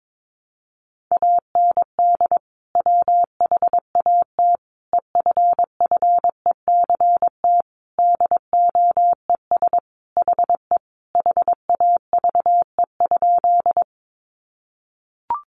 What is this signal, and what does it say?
Morse code, heard as a keyed tone, about 22 wpm, 700 Hz. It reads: AND WHAT EFFECT DOES HE HAVE?
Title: Sentences from Top 1000 Words 22wpm